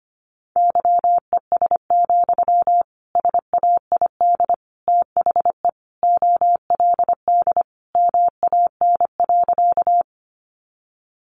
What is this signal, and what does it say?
Morse code, heard as a keyed tone, about 25 wpm, 700 Hz. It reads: YES, SAID THE OLD MAN.
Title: Sentences from Top 500 Words 25wpm